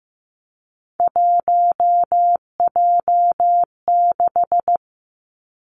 Morse code, heard as a keyed tone, about 15 wpm, 700 Hz. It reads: 1J6
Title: Three Challenging Letter-Number Combinations 15wpm